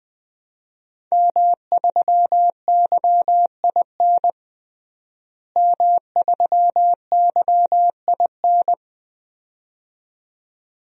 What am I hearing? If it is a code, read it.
Morse code, heard as a keyed tone, about 20 wpm, 700 Hz. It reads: M3YIN M3YIN
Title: DXCC-100 Call Signs - 5 Characters - Repeated 2x - 20wpm